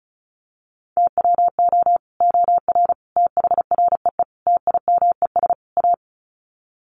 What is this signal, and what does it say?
Morse code, heard as a keyed tone, about 34 wpm, 700 Hz. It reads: TWO OR THREE TIMES A
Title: Sets of 5 Words Encore2 35wpm